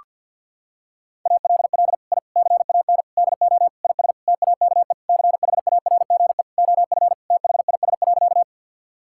Morse code, heard as a keyed tone, about 50 wpm, 700 Hz. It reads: ALL I CAN DO IS TAKE CHARGE OF THIS.
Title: Sentences from Top 1000 Words - Review Part 6 - 50wpm